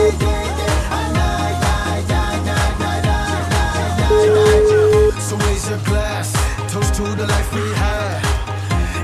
Ringback tone (Europe, 425 Hz), with noise.